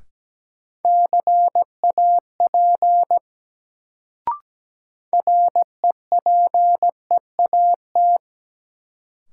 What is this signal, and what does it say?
Morse code, heard as a keyed tone, about 17 wpm, 700 Hz. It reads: CAP REPEAT
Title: Sentences from Top 1600 Words - New Word Drill Alt - 17wpm